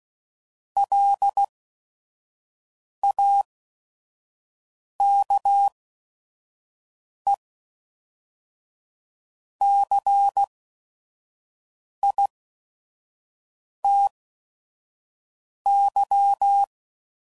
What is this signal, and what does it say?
Morse code, heard as a keyed tone, about 16 wpm, spaced as for 5 wpm, 800 Hz. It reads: LAKECITY
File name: CW Code Groups @5WPM / fromSM's / Cities